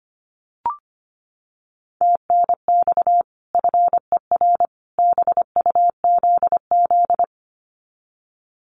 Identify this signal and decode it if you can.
Morse code, heard as a keyed tone, about 25 wpm, 700 Hz. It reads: TNX FER BUZZ
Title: Common QSO Phrases 25wpm